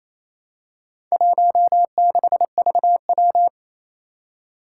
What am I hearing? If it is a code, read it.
Morse code, heard as a keyed tone, about 29 wpm, 700 Hz. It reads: 16VW
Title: Four Challenging Letter-Number Combinations 28wpm